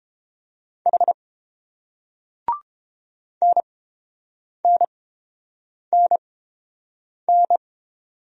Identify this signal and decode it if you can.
Morse code, heard as a keyed tone, about 30 wpm, spaced as for 7 wpm, 700 Hz. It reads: H NNNN